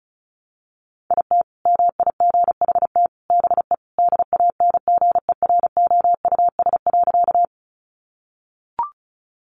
Morse code, heard as a keyed tone, about 34 wpm, 700 Hz. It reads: IT MIGHT BE DANGEROUS.
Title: Sentences from Top 1200 Words 35wpm